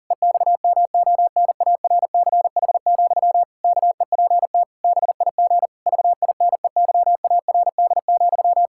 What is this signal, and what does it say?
Morse code, heard as a keyed tone, about 37 wpm, 700 Hz. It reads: EXMONARCH, KEPT BIG VINEYARD,
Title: Pangram Challenge v2 - 40wpm